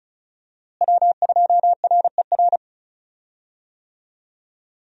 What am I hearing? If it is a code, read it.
Morse code, heard as a keyed tone, about 36 wpm, 700 Hz. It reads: W2RER
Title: USA Call Signs - 5 Characters - Repeated 2x - 35wpm